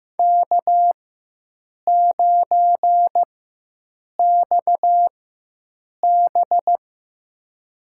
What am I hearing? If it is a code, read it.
Morse code, heard as a keyed tone, about 15 wpm, spaced as for 7 wpm, 700 Hz. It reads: K9XB
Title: USA Call Signs - 4 Characters - 4x Character Spacing - 15wpm